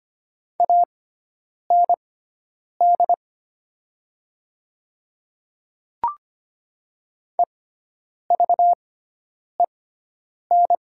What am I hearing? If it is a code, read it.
Morse code, heard as a keyed tone, about 25 wpm, spaced as for 9 wpm, 700 Hz. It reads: AND EVEN
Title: 6x Character Spacing - Top 200 Words - 25wpm